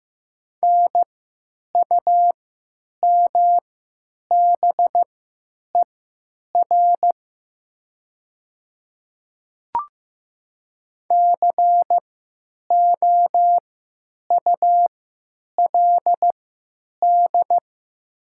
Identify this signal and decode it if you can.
Morse code, heard as a keyed tone, about 15 wpm, spaced as for 9 wpm, 700 Hz. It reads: NUMBER COULD
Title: 3x Character Spacing - Top 200 Words - 15wpm